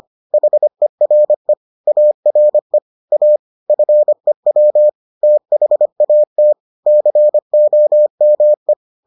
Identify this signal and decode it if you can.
Morse code, heard as a keyed tone, about 25 wpm, 600 Hz. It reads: HERE ARE A FEW THAT COME